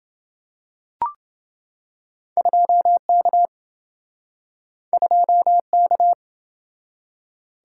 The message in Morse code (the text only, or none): 2K 2K